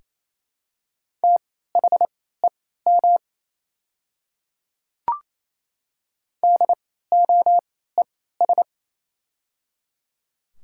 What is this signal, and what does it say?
Morse code, heard as a keyed tone, about 28 wpm, spaced as for 16 wpm, 700 Hz. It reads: THEM DOES
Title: 3x Character Spacing - Top 200 Words - 28wpm